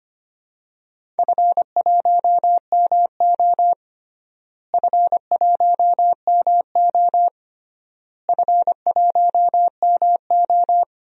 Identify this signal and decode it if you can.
Morse code, heard as a keyed tone, about 25 wpm, 700 Hz. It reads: F1MO F1MO F1MO
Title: DXCC-100 Call Signs - 4 Characters - Repeated 3x - 25wpm